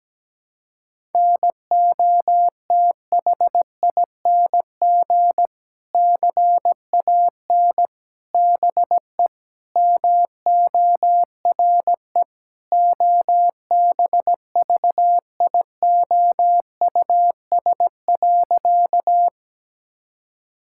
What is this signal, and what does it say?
Morse code, heard as a keyed tone, about 17 wpm, 700 Hz. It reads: NOTHING CAN BE MORE OBVIOUS.
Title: Sentences from Top 1600 Words 17wpm